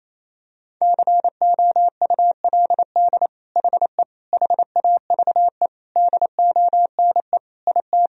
Morse code, heard as a keyed tone, about 28 wpm, 700 Hz. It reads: COULD HE HAVE DONE IT
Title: Sentences from Top 1000 Words - Review Part 3 - 28wpm